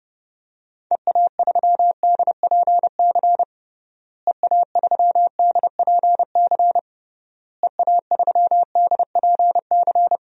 Morse code, heard as a keyed tone, about 30 wpm, 700 Hz. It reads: EA3DPC EA3DPC EA3DPC
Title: DXCC-100 Call Signs - 6 Characters - Repeated 3x - 30wpm